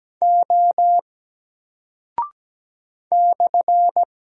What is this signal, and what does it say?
Morse code, heard as a keyed tone, about 17 wpm, 700 Hz. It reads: O /
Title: Single Character (Letter:Number:Punctuation:Prosign) 17wpm